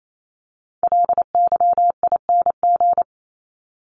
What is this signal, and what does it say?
Morse code, heard as a keyed tone, about 28 wpm, 700 Hz. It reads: LYING
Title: ING Suffix - Top 100 Words with Base Words 28wpm